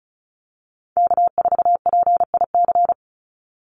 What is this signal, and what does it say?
Morse code, heard as a keyed tone, about 34 wpm, 700 Hz. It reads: K4PIC
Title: NA SOTA Chaser Callsigns 35wpm